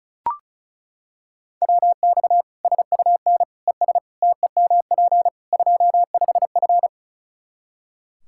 Morse code, heard as a keyed tone, about 36 wpm, 700 Hz. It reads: WX SUN ES TEMP 25F
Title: Practice QSO - Exchange 2 - Info - 35wpm